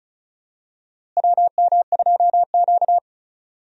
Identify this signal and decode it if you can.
Morse code, heard as a keyed tone, about 36 wpm, 700 Hz. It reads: WM2Q
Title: USA Call Signs - 4 Characters - Repeated 2x - 35wpm